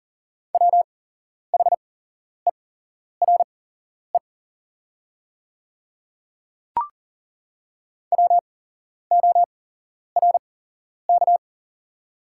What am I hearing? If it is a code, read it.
Morse code, heard as a keyed tone, about 38 wpm, spaced as for 11 wpm, 700 Hz. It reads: WHERE WORK
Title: 8x Character Spacing - Top 200 Words - 40wpm